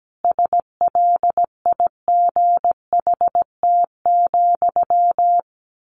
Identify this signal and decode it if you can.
Morse code, heard as a keyed tone, about 17 wpm, 700 Hz. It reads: SLIGHT,
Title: A Sound Of Thunder 17wpm